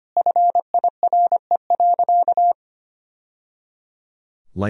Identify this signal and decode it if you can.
Morse code, heard as a keyed tone, about 25 wpm, 700 Hz. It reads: FIRE.
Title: Sentences from Top 1000 Words - Review Part 1 - 25wpm